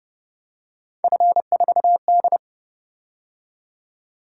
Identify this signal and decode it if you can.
Morse code, heard as a keyed tone, about 30 wpm, 700 Hz. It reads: F4D